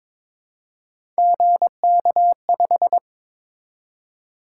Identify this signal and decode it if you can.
Morse code, heard as a keyed tone, about 21 wpm, 700 Hz. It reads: GK5
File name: practice / 22wpm / Three Letter-Number Combinations - 22wpm